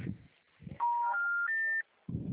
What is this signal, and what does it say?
Special information tone, with noise.